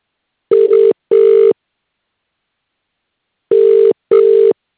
Ringback tone (UK), 400 + 450 Hz.